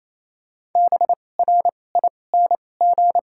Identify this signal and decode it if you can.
Morse code, heard as a keyed tone, about 28 wpm, spaced as for 20 wpm, 700 Hz. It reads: BRING
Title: 2x Character Spacing - Top 500 Words - 28wpm